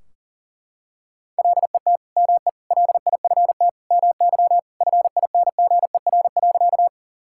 Morse code, heard as a keyed tone, about 40 wpm, 700 Hz. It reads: LET ME LIFT MY FINGER.
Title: Sentences from Top 1600 Words 40wpm